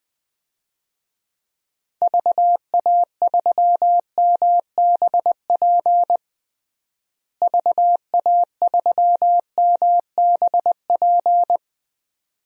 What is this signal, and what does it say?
Morse code, heard as a keyed tone, about 20 wpm, 700 Hz. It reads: VA3MBP VA3MBP
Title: DXCC-100 Call Signs - 6 Characters - Repeated 2x - 20wpm